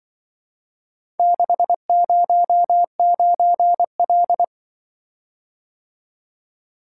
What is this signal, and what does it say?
Morse code, heard as a keyed tone, about 24 wpm, 700 Hz. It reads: 609L
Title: Speed-Racing - Four Letter-Number Combinations - 20wpm